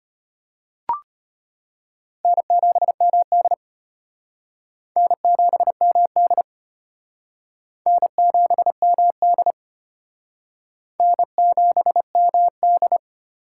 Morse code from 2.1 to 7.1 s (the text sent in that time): N7MD N7MD